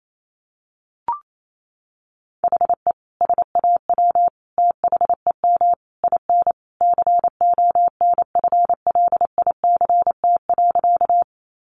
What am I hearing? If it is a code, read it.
Morse code, heard as a keyed tone, about 28 wpm, 700 Hz. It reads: HE SAW THEM IN CONFLICT.
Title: Sentences from Top 1300 Words 28wpm